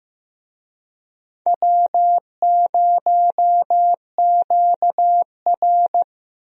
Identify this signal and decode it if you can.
Morse code, heard as a keyed tone, about 16 wpm, 700 Hz. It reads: W0QR
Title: USA Call Signs - 4 Characters - Repeated 2x - 15wpm